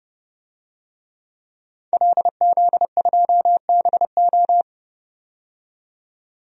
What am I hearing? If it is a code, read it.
Morse code, heard as a keyed tone, about 30 wpm, 700 Hz. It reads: LZ2BO
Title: DXCC-100 Call Signs - 5 Characters - 30wpm